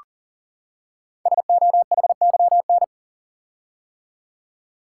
Morse code, heard as a keyed tone, about 39 wpm, 690 Hz. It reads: SOHYN
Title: Code Groups - 5 Letters - 40wpm